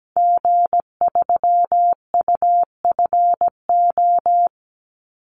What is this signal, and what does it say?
Morse code, heard as a keyed tone, about 17 wpm, 700 Hz. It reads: G3UFO
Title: Call Signs - 17wpm